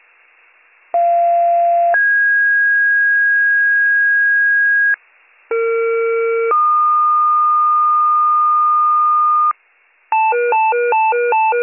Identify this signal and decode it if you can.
Two-tone page, heard as a one-second tone, then a three-second tone, about 690 then 1650 Hz, about 460 then 1220 Hz, with noise. It is heard from 0.9 to 9.5 s.